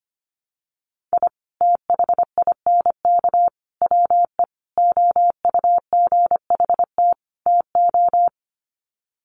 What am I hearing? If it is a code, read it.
Morse code, heard as a keyed tone, about 25 wpm, 700 Hz. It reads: I THINK WE OUGHT TO
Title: Sets of 5 Words Encore 25wpm